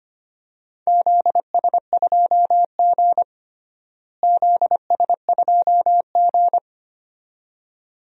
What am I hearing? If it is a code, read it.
Morse code, heard as a keyed tone, about 25 wpm, 700 Hz. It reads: ZS2G ZS2G
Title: DXCC-100 Call Signs - 4 Characters - Repeated 2x - 25wpm